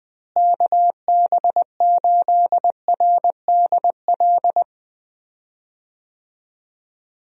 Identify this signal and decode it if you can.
Morse code, heard as a keyed tone, about 20 wpm, 700 Hz. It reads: KB8RDL
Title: USA Call Signs - 6 Characters - Repeated 2x - 20wpm